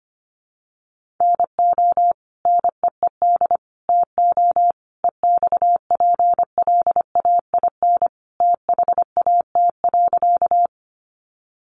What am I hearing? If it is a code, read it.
Morse code, heard as a keyed tone, about 25 wpm, 700 Hz. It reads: NO NEED TO EXPLAIN THAT.
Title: Sentences from Top 1100 Words 25wpm